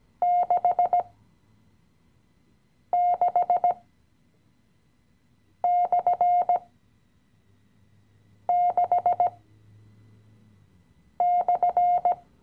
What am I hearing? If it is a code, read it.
Morse code, heard as a keyed tone, about 16 wpm, spaced as for 4 wpm, 710 Hz. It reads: 66/6/
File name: CW-Online-Kurs / Anfänger / Lektion 016